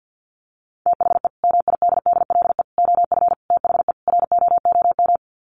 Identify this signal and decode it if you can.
Morse code, heard as a keyed tone, about 50 wpm, 700 Hz. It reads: THE MIDDLE OF THE ROOM
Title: Sets of 5 Words Encore2 50wpm